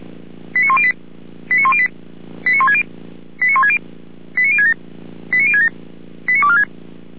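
Five-tone selective call (ZVEI), heard as quick runs of five tones, tones of about 70 ms. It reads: 89108 89108 89160 89160 89067 89067 89246
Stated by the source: ZVEI five-tone call 89078 received via SDR.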